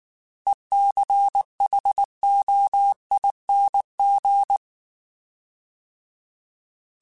Morse code, heard as a keyed tone, about 19 wpm, 800 Hz. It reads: ECHOING
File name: Alphabetical / Alphabetical mp3s / Es @18